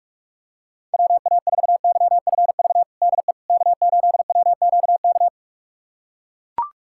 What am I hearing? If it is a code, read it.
Morse code, heard as a keyed tone, about 42 wpm, 700 Hz. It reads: WA4YFV DE K8WQK
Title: USA Callsign DE Callsign 45wpm